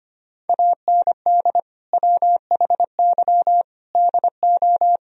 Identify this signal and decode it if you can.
Morse code, heard as a keyed tone, about 25 wpm, 700 Hz. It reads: AND WHY DO
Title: Sentences from Top 1000 Words - Review Part 4 - 25wpm